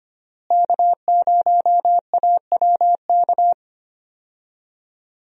Morse code, heard as a keyed tone, about 25 wpm, 700 Hz. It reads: K0AWK
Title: USA Call Signs - 5 Characters - Repeated 3x - 25wpm